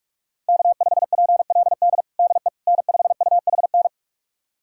Morse code, heard as a keyed tone, about 46 wpm, 700 Hz. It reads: K5PLD DE N5UHN
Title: USA Callsign DE Callsign - 5 Characters - 45wpm